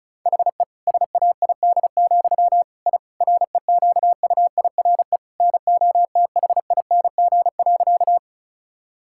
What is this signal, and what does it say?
Morse code, heard as a keyed tone, about 34 wpm, 700 Hz. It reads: HE SAID, I REQUIRE NOTHING.
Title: Sentences from Top 1000 Words - Review Part 6 - 35wpm